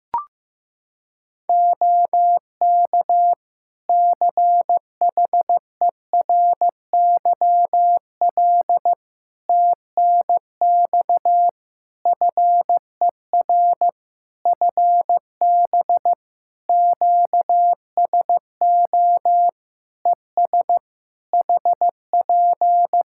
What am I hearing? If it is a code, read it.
Morse code, heard as a keyed tone, about 15 wpm, 700 Hz. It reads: OK CHERYL TNX FER FB QSO ES HP
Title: Practice QSO - Exchange 4 - Ending - 15wpm